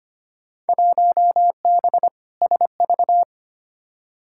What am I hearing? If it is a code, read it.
Morse code, heard as a keyed tone, about 25 wpm, 700 Hz. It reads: 1B SV
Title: Field Day Exchanges Encore - 25wpm